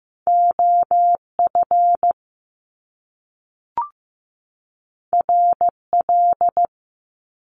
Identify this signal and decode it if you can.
Morse code, heard as a keyed tone, about 15 wpm, 700 Hz. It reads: OF RL
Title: Top Two Letter Combinations in Words 15wpm